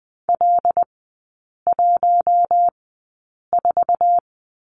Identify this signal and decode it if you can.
Morse code, heard as a keyed tone, about 20 wpm, spaced as for 8 wpm, 700 Hz. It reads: L14
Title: 074 Lesson 13e - Letter L - Three Characters 20wpm